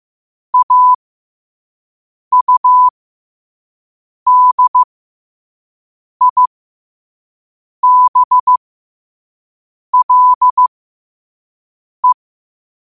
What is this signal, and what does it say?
Morse code, heard as a keyed tone, about 15 wpm, spaced as for 5 wpm, 1000 Hz. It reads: AUDIBLE
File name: qotdfiles / 5wpm